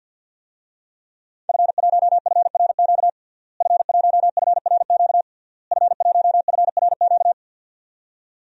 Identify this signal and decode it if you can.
Morse code, heard as a keyed tone, about 51 wpm, 700 Hz. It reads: F1FRQ F1FRQ F1FRQ
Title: DXCC-100 Call Signs - 5 Characters - Repeated 3x - 50wpm